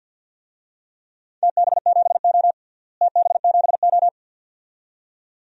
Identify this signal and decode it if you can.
Morse code, heard as a keyed tone, about 52 wpm, 700 Hz. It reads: TB7O TB7O
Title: DXCC-100 Call Signs - 4 Characters - Repeated 2x - 50wpm